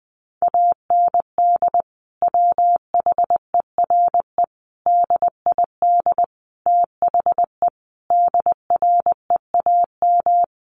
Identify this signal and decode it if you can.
Morse code, heard as a keyed tone, about 20 wpm, 700 Hz. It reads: AND WHERE DID THE DREAM
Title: Sentences from Top 1500 Words 20wpm